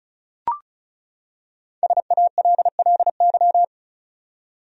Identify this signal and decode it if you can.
Morse code, heard as a keyed tone, about 34 wpm, 700 Hz. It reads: SALLY
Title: USA Common Names - Max 5 Letters - 35wpm